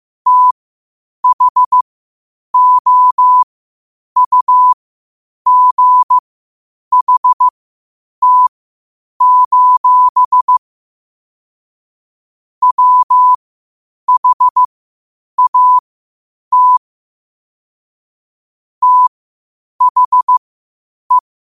Morse code, heard as a keyed tone, about 15 wpm, spaced as for 8 wpm, 1000 Hz. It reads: THOUGHT: WHAT THE
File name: qotdfiles / 8wpm